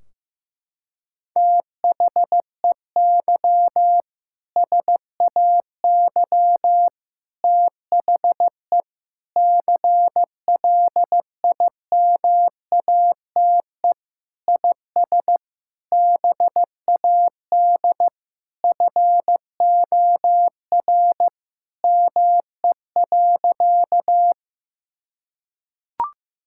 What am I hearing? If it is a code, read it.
Morse code, heard as a keyed tone, about 15 wpm, 700 Hz. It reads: THEY SAY THE CLIMATE IS BAD FOR ME.
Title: Sentences from Top 1600 Words 15wpm